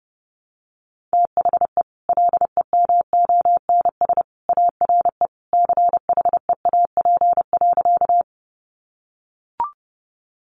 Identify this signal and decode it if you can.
Morse code, heard as a keyed tone, about 30 wpm, 700 Hz. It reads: THE LEMONS ARE CHEAP.
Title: Additional Easy Phrases 30wpm